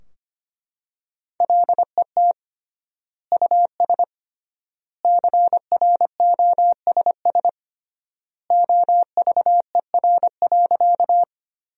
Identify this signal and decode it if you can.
Morse code, heard as a keyed tone, about 25 wpm, 700 Hz. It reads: LET US CROSS OVER.
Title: 3x Word-Spacing - Sentences from Top 300 Words - 25wpm